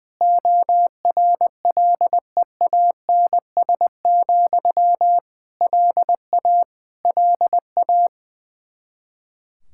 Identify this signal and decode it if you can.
Morse code, heard as a keyed tone, about 21 wpm, 700 Hz. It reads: ORLEANS, LA LA
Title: Practice QSO - Exchange 1 - Beginning - 20wpm